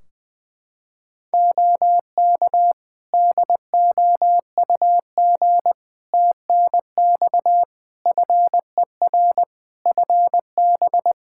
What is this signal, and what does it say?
Morse code, heard as a keyed tone, about 20 wpm, 700 Hz. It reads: OK DOUG TNX FER FB
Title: K1USN SST QSO Practice 20wpm